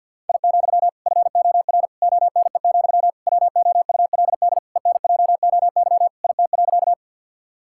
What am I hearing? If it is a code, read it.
Morse code, heard as a keyed tone, about 46 wpm, 700 Hz. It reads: I, FOR ONE, WOULD ENJOY IT.
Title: Sentences from Top 1000 Words - Review Part 5 - 50wpm